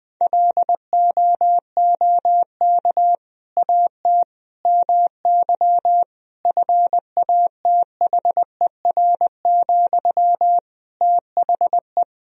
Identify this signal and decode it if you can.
Morse code, heard as a keyed tone, about 20 wpm, 700 Hz. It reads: LOOK AT MY FATHER, THE
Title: Sentences from Top 1000 Words - Review Part 3 - 20wpm